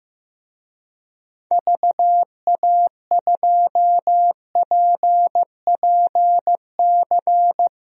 Morse code, heard as a keyed tone, about 15 wpm, 700 Hz. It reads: VA2PPC
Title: DXCC-100 Call Signs - 6 Characters - 15wpm